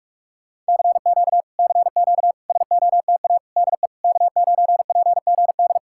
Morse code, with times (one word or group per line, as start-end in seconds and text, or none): 0.7-1.4 CQ
1.6-2.3 CQ
2.5-3.4 SOTA
3.6-3.9 DE
4.0-5.8 K9PGD